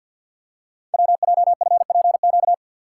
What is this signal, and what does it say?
Morse code, heard as a keyed tone, about 50 wpm, 700 Hz. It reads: WJFPQ
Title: Code Groups - 5 Letters - 50wpm